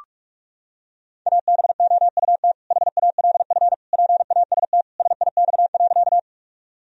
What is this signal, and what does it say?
Morse code, heard as a keyed tone, about 42 wpm, 700 Hz. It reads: ABOUT HALF PAST SIX.
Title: Sentences from Top 1000 Words - Review Part 3 - 45wpm